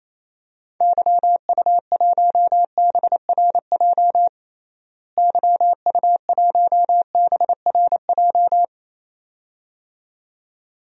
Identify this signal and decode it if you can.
Morse code, heard as a keyed tone, about 27 wpm, 700 Hz. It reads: YU1BRJ YU1BRJ
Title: DXCC-100 Call Signs - 6 Characters - Repeated 2x - 28wpm